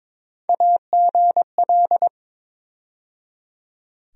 Morse code, heard as a keyed tone, about 22 wpm, 700 Hz. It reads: AGL
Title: USA Call Sign Postfixes - 3 Letters 22wpm